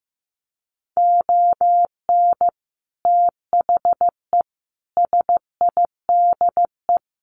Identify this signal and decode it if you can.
Morse code, heard as a keyed tone, about 15 wpm, 700 Hz. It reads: ON THE SIDE
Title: Sets of 5 Words 15wpm